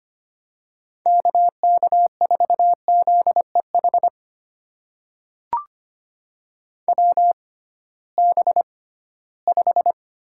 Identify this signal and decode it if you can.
Morse code, heard as a keyed tone, about 25 wpm, spaced as for 16 wpm, 700 Hz. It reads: KK4ZEH WB5
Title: USA Call Signs - 6 Characters - 6x Character Spacing - 25wpm